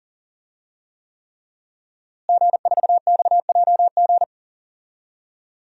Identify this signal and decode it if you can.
Morse code, heard as a keyed tone, about 40 wpm, 700 Hz. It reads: G4XJG
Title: DXCC-100 Call Signs - 5 Characters - Repeated 3x - 40wpm